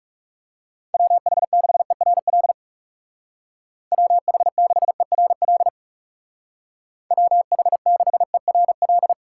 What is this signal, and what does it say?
Morse code, heard as a keyed tone, about 38 wpm, 700 Hz. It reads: WH6ERL WH6ERL WH6ERL